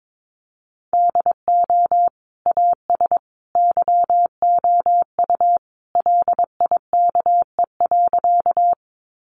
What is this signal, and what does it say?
Morse code, heard as a keyed tone, about 22 wpm, 700 Hz. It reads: DO AS YOU LIKE.
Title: Sentences from Top 100 Words 22wpm